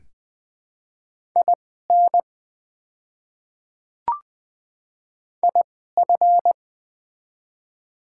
Morse code, heard as a keyed tone, about 20 wpm, spaced as for 15 wpm, 690 Hz. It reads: IN IF
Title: 2x Character Spacing - Top 100 Words - 20wpm